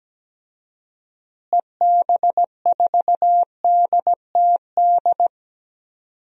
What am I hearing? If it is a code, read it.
Morse code, heard as a keyed tone, about 17 wpm, 700 Hz. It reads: EB4DTD
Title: DXCC-100 Call Signs - 6 Characters - Repeated 3x - 17wpm